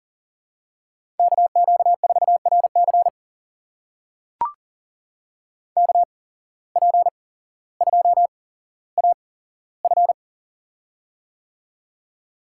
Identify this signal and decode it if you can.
Morse code, heard as a keyed tone, about 39 wpm, spaced as for 17 wpm, 700 Hz. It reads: KQ4RC KP2AF
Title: USA Call Signs - 5 Characters - 8x Character Spacing - 40wpm